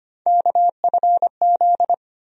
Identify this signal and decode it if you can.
Morse code, heard as a keyed tone, about 25 wpm, 700 Hz. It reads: KFZ
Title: USA Call Sign Postfixes - 3 Letters 25wpm